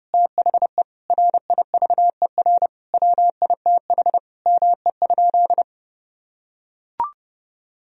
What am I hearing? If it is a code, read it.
Morse code, heard as a keyed tone, about 30 wpm, 700 Hz. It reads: THE RIVER WITH ME?
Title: Sentences from Top 1000 Words - Review Part 3 - 30wpm